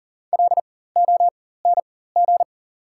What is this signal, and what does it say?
Morse code, heard as a keyed tone, about 39 wpm, spaced as for 19 wpm, 700 Hz. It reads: LONG
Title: 4x Character Spacing - Top 200 Words - 40wpm